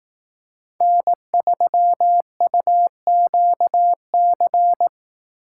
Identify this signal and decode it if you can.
Morse code, heard as a keyed tone, about 18 wpm, 700 Hz. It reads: N3UQC
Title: Speed Racing - USA 1x3 Call Signs 15wpm